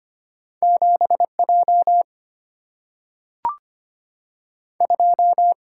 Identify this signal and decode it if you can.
Morse code, heard as a keyed tone, about 25 wpm, 700 Hz. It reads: 7 J 2
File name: Two Letter-Number Combinations - 25wpm